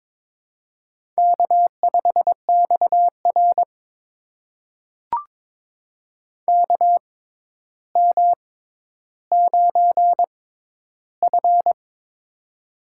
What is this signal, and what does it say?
Morse code, heard as a keyed tone, about 22 wpm, spaced as for 11 wpm, 700 Hz. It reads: K5XR KM9F
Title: USA Call Signs - 4 Characters - 6x Character Spacing - 22wpm